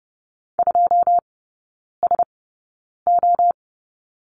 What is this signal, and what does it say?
Morse code, heard as a keyed tone, about 30 wpm, spaced as for 9 wpm, 700 Hz. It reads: 2SO
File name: course / 30:10wpm / 087 Lesson 15e - Number 5 - Three Characters 30wpm